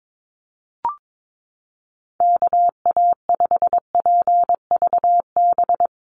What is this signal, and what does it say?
Morse code, heard as a keyed tone, about 22 wpm, 700 Hz. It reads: KA5PVB
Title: NA SOTA Chaser Callsigns 22wpm